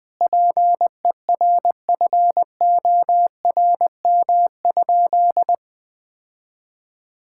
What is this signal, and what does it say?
Morse code, heard as a keyed tone, about 19 wpm, 700 Hz. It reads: PERFORM?
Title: Sentences from Top 1000 Words - Review Part 6 - 20wpm